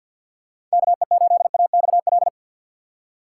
Morse code, heard as a keyed tone, about 47 wpm, 700 Hz. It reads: KE8AXL
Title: USA Call Signs - 6x Character Spacing - 50wpm